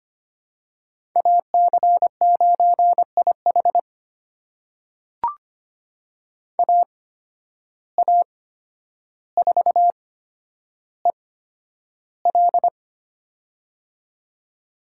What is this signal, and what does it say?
Morse code, heard as a keyed tone, about 25 wpm, spaced as for 11 wpm, 700 Hz. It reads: AC9IH AA4EL
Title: USA Call Signs - 5 Characters - 8x Character Spacing - 25wpm